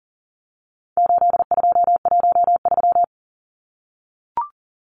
Morse code, heard as a keyed tone, about 40 wpm, 700 Hz. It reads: 8213
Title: Numbers - 4-Digit 40wpm